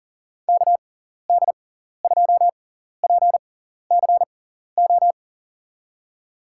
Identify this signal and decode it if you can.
Morse code, heard as a keyed tone, about 40 wpm, spaced as for 14 wpm, 700 Hz. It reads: KD2PCO
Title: USA Call Signs - 6 Characters - 6x Character Spacing - 40wpm